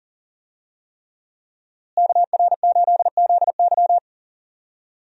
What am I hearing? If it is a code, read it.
Morse code, heard as a keyed tone, about 40 wpm, 700 Hz. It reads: KR8ZY
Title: USA Call Signs - 5 Characters - Repeated 3x - 40wpm